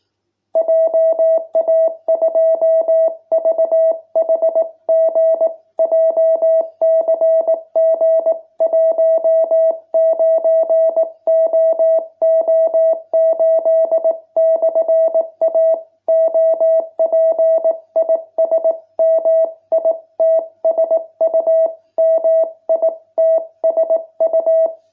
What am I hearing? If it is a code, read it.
Morse code, heard as a keyed tone, about 18 wpm, 660 Hz. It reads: JA2VHG JCG19OO8/A OP IS MITSU MITSU